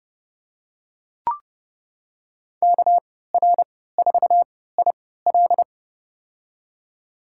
Morse code, heard as a keyed tone, about 30 wpm, spaced as for 17 wpm, 700 Hz. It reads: KR4IL
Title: USA Call Signs - 5 Characters - 3x Character Spacing - 30wpm